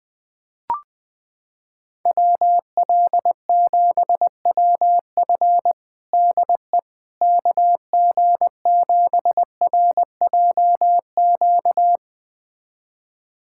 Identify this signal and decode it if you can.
Morse code, heard as a keyed tone, about 20 wpm, 700 Hz. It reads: WL7WF DE KG7RJQ
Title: USA Callsign DE Callsign 20wpm